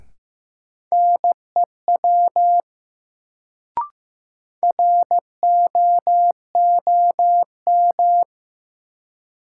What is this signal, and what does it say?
Morse code, heard as a keyed tone, about 15 wpm, 700 Hz. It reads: NEW ROOM
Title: Sentences from Top 300 Words - New Word Drill - 15wpm